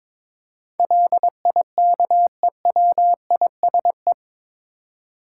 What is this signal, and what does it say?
Morse code, heard as a keyed tone, about 22 wpm, 700 Hz. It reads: LIKEWISE
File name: practice / 22wpm / Eight Letter Words - 22wpm